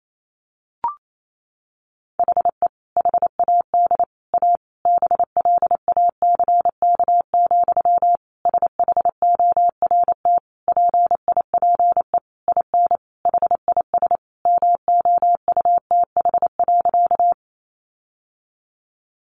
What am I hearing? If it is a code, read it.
Morse code, heard as a keyed tone, about 28 wpm, 700 Hz. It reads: HE HAD A BLACK, SHORT PIPE IN HIS MOUTH.
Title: Sentences from Top 1500 Words 28wpm